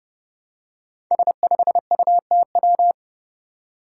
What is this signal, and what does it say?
Morse code, heard as a keyed tone, about 30 wpm, 700 Hz. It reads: S5UTW
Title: Five Challenging Letter-Number Combinations 30wpm